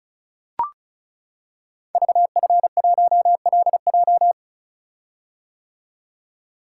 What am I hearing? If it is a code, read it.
Morse code, heard as a keyed tone, about 36 wpm, 700 Hz. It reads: VF1LJ
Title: Code Groups - 5 Letter:Numbers - 35wpm